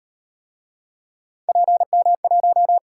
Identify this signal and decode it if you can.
Morse code, heard as a keyed tone, about 38 wpm, 700 Hz. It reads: PM1